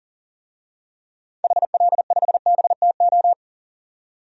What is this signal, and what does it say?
Morse code, heard as a keyed tone, about 40 wpm, 700 Hz. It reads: HL5BTO
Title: DXCC-100 Call Signs - 6 Characters - 40wpm